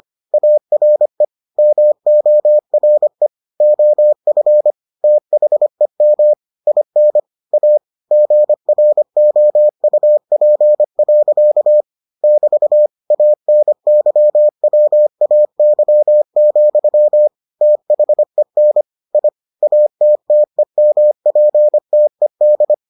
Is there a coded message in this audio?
Yes: Morse code from 0.3 to 22.7 s, heard as a keyed tone, about 25 wpm, 600 Hz. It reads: ARE MORE OF THEM IN A GROUP. <BT> ANYWAY, THEN I ATTEMPTED